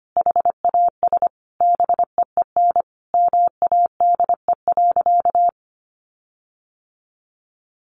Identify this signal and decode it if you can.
Morse code, heard as a keyed tone, about 25 wpm, 700 Hz. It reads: HAS BEEN MADE.
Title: Sentences from Top 1500 Words 25wpm